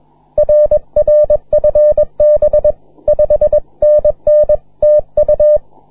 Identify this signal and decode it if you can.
Morse code, heard as a keyed tone, about 22 wpm, 560 Hz. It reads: RR FB 5NN TU
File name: wp / cx2aq 12m cw 21 mar 2010 1419z